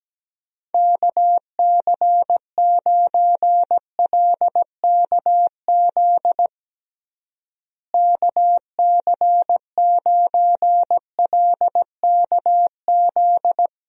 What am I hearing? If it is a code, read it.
Morse code, heard as a keyed tone, about 17 wpm, 700 Hz. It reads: KC9LKZ KC9LKZ
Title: USA Call Signs - 6 Characters - Repeated 2x - 17wpm